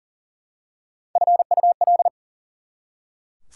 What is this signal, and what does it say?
Morse code, heard as a keyed tone, about 40 wpm, 700 Hz. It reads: FUL
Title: Prefixes and Suffixes 40wpm